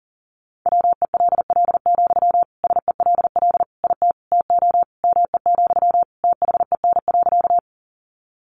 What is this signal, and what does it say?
Morse code, heard as a keyed tone, about 40 wpm, 700 Hz. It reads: WELL, SELL IT TO ME, THEN.
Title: Sentences from Top 1100 Words 40wpm